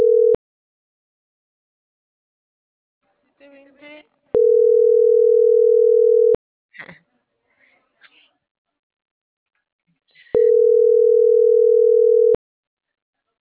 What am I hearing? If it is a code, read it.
Ringback tone (North America), 440 + 480 Hz.